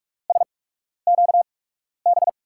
Morse code, heard as a keyed tone, about 44 wpm, spaced as for 12 wpm, 700 Hz. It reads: SQB
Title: USA Call Signs - 6 Characters - 8x Character Spacing - 45wpm